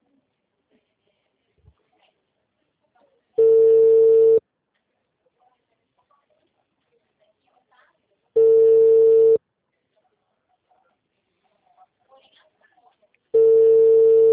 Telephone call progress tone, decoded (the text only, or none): ringback tone